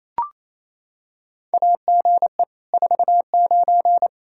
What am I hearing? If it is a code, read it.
Morse code, heard as a keyed tone, about 27 wpm, 700 Hz. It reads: AGE 49
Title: K1USN SST QSO Practice 28wpm